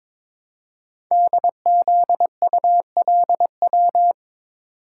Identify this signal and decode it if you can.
Morse code, heard as a keyed tone, about 22 wpm, 700 Hz. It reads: DZULW
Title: Code Groups - 5 Letters - 22wpm